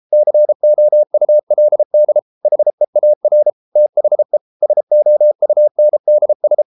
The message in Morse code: COULD HEAR THE SOUNDS

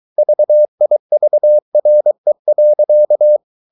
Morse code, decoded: VIVRE.

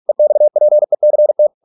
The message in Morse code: EXPECT

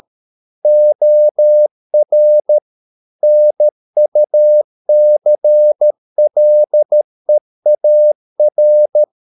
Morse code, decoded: OR NUCLEAR

